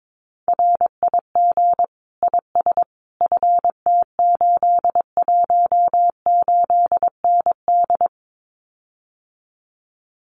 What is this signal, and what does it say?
Morse code, heard as a keyed tone, about 22 wpm, 700 Hz. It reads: RIG IS FT818ND
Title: Common QSO Phrases Encore 22wpm